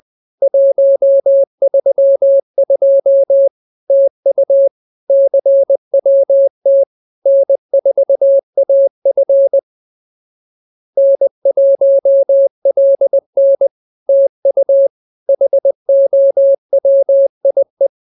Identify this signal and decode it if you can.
Morse code, heard as a keyed tone, about 20 wpm, 550 Hz. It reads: 132 TU CWT N4AF N1LN TU HOWIE